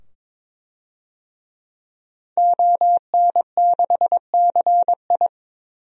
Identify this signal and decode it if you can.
Morse code, heard as a keyed tone, about 22 wpm, 700 Hz. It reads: ON6CI
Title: DXCC-100 Call Signs - 5 Characters - 22wpm